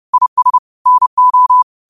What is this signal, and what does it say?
Morse code, heard as a keyed tone, about 30 wpm, 1000 Hz. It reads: IS NO
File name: qotdfiles / 30wpm